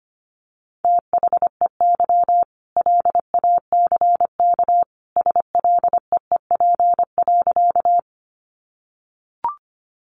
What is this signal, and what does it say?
Morse code, heard as a keyed tone, about 25 wpm, 700 Hz. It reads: THEY LACK SLEEP.
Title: Sentences from Top 1100 Words 25wpm